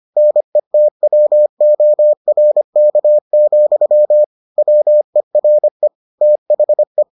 Morse code, heard as a keyed tone, about 25 wpm, 600 Hz. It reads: NETWORK, WERE THE